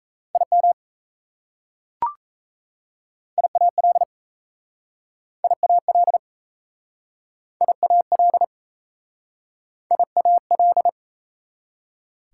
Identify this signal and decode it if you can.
Morse code, heard as a keyed tone, about 37 wpm, 700 Hz. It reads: IM IAL IAL IAL IAL